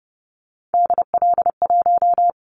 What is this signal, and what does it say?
Morse code, heard as a keyed tone, about 30 wpm, 700 Hz. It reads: DL1
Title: Callsign Prefixes for Top 50 DXCC - 30wpm